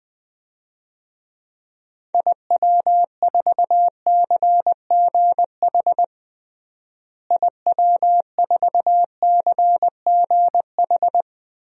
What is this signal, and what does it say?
Morse code, heard as a keyed tone, about 20 wpm, 700 Hz. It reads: IW4CGH IW4CGH
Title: DXCC-100 Call Signs - 6 Characters - Repeated 3x - 20wpm